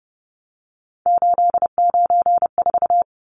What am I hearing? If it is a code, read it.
Morse code, heard as a keyed tone, about 30 wpm, 700 Hz. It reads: 894
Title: Numbers - 3-Digit 30wpm